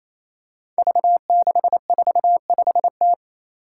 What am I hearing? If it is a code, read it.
Morse code, heard as a keyed tone, about 28 wpm, 700 Hz. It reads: V645T
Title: Five Challenging Letter-Number Combinations 28wpm